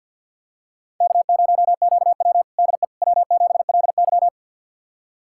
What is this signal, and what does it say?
Morse code, heard as a keyed tone, about 52 wpm, 700 Hz. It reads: K0QW DE W7LY
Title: USA Callsign DE Callsign - 4 Characters - 50wpm